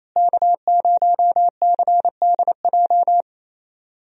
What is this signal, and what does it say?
Morse code, heard as a keyed tone, about 28 wpm, 700 Hz. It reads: K0CDJ
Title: USA Call Signs - 5 Characters - Repeated 2x - 28wpm